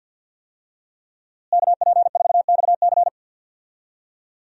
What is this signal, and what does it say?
Morse code, heard as a keyed tone, about 47 wpm, 700 Hz. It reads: KP4XC
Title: DXCC-100 Call Signs - 5 Characters - Repeated 3x - 50wpm